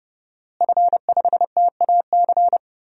Morse code, heard as a keyed tone, about 32 wpm, 700 Hz. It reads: F5TAC
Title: DXCC-100 Call Signs - 5 Characters - Repeated 2x - 30wpm